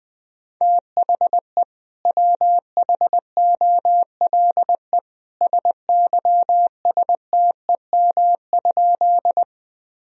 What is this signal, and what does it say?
Morse code, heard as a keyed tone, about 20 wpm, 700 Hz. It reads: THE WHOLE SYSTEM?
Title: Sentences from Top 1000 Words - Review Part 3 - 20wpm